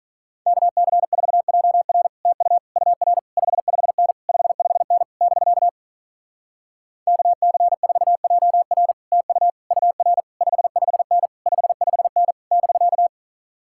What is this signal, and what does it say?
Morse code, heard as a keyed tone, about 46 wpm, 700 Hz. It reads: KC4JR TU UR 55N 55N <BK> KC4JR TU UR 55N 55N <BK>